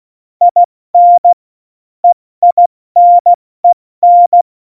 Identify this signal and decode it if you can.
Morse code, heard as a keyed tone, about 16 wpm, spaced as for 14 wpm, 700 Hz. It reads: IN EINEN
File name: Morsekurs / Witze / Witz 05. Januar 2021, 70 F80